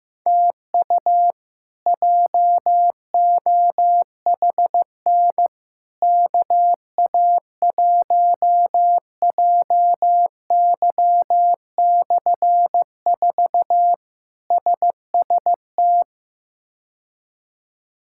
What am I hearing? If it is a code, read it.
Morse code, heard as a keyed tone, about 16 wpm, 700 Hz. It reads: TU JOHN KA1JY/4 SST
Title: K1USN SST QSO Practice - 15wpm